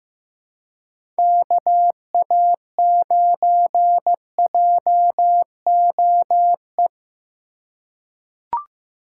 Morse code, heard as a keyed tone, about 15 wpm, 700 Hz. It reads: KA9JOE
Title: USA Call Signs - 2x Character Spacing - 15wpm